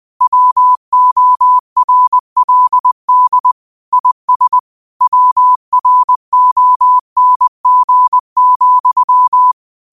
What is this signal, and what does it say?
Morse code, heard as a keyed tone, about 20 wpm, 1000 Hz. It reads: WORLD IS WRONG,